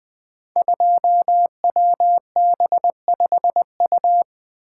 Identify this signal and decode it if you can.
Morse code, heard as a keyed tone, about 20 wpm, 700 Hz. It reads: 2WB5U
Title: Code Groups - 5 Letter:Numbers - 20wpm